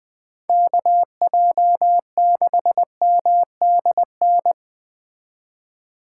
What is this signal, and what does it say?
Morse code, heard as a keyed tone, about 21 wpm, 700 Hz. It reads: KJ6MDN